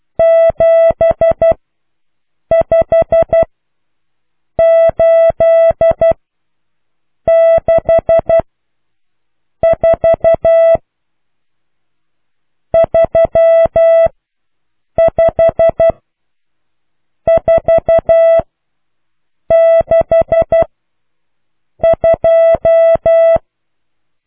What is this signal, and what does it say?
Morse code, heard as a keyed tone, about 12 wpm, spaced as for 6 wpm, 660 Hz. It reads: 75864 35462